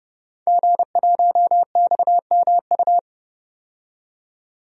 Morse code, heard as a keyed tone, about 30 wpm, 700 Hz. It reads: G1XMU
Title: DXCC-100 Call Signs - 5 Characters - 30wpm